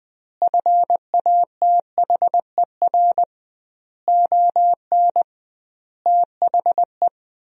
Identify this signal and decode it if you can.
Morse code, heard as a keyed tone, about 20 wpm, 700 Hz. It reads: FATHER ON THE
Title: 2x Word-Spacing - Sentences from Top 500 Words - 20wpm